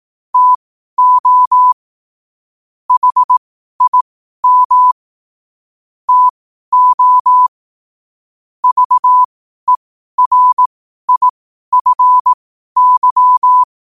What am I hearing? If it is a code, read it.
Morse code, heard as a keyed tone, about 18 wpm, spaced as for 12 wpm, 1000 Hz. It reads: TO HIM TO VERIFY
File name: qotdfiles / 12wpm